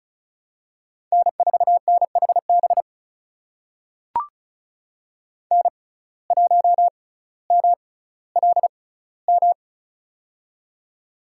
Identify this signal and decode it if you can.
Morse code, heard as a keyed tone, about 36 wpm, spaced as for 18 wpm, 700 Hz. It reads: N4NHB N1MLM
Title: USA Call Signs - 6x Character Spacing - 35wpm